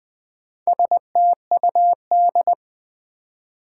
Morse code, heard as a keyed tone, about 20 wpm, 700 Hz. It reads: STUD
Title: Build A Word - 1 Letter At A Time - 5-Letter Words - 20wpm